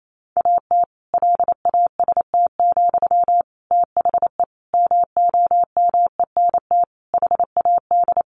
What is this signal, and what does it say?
Morse code, heard as a keyed tone, about 28 wpm, 700 Hz. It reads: AT LAST, THE MOMENT HAD
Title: Sentences from Top 1500 Words 28wpm